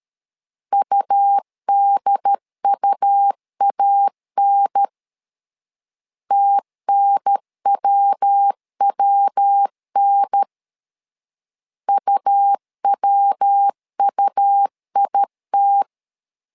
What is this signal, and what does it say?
Morse code, heard as a keyed tone, about 13 wpm, 800 Hz. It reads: UDUAN TNWWN UWUIT
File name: LEZIONI / 3 MWRO / 3 MW 12wpm 1633